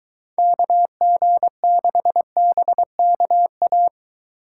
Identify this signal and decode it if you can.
Morse code, heard as a keyed tone, about 23 wpm, 700 Hz. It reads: KG6BKA